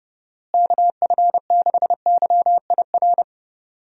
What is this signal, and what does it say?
Morse code, heard as a keyed tone, about 30 wpm, 700 Hz. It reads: KF6YIR
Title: USA Call Signs - 6 Characters - Repeated 2x - 30wpm